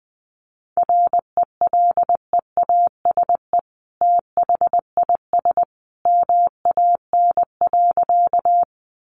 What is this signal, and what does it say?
Morse code, heard as a keyed tone, about 19 wpm, 700 Hz. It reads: RELEASE THIS MAN.
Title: Sentences from Top 1100 Words 20wpm